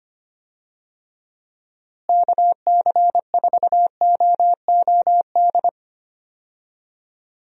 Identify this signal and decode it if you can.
Morse code, heard as a keyed tone, about 25 wpm, 700 Hz. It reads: KC4OOD
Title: USA Call Signs - 6 Characters - Repeated 3x - 25wpm